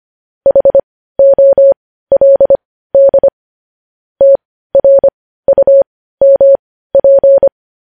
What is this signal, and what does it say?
Morse code, heard as a keyed tone, about 25 wpm, spaced as for 15 wpm, 550 Hz. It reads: HOLD TRUMP
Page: news headlines in morse code at 15 wpm